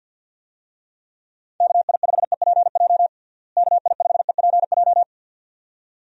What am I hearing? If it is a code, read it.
Morse code, heard as a keyed tone, about 51 wpm, 700 Hz. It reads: KI5EPJ KI5EPJ
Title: USA Call Signs - 6 Characters - Repeated 2x - 50wpm